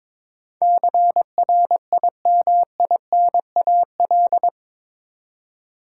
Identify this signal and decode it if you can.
Morse code, heard as a keyed tone, about 22 wpm, 700 Hz. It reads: CRIMINAL